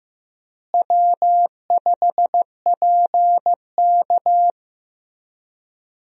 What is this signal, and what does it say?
Morse code, heard as a keyed tone, about 15 wpm, 700 Hz. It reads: W5PK
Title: USA Call Signs - 4 Characters - Repeated 3x - 15wpm